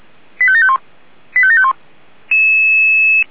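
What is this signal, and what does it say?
Five-tone selective call (ZVEI), heard as quick runs of five tones, tones of about 70 ms. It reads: 85751 85751
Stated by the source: ZVEI five-tone selective call received by SDR.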